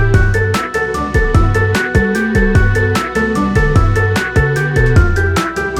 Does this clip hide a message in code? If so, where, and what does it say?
No.